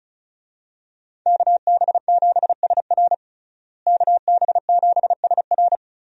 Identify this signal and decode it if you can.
Morse code, heard as a keyed tone, about 34 wpm, 700 Hz. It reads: KB7SR KB7SR
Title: USA Call Signs - 5 Characters - Repeated 3x - 35wpm